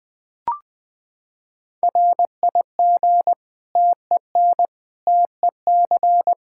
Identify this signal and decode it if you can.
Morse code, heard as a keyed tone, about 20 wpm, 700 Hz. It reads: RIG TEN TEC
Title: Practice QSO - Exchange 2 - Info - 20wpm